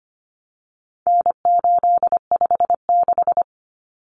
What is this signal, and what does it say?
Morse code, heard as a keyed tone, about 25 wpm, 700 Hz. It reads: N856
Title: Common Cut Numbers - 4-Digit 25wpm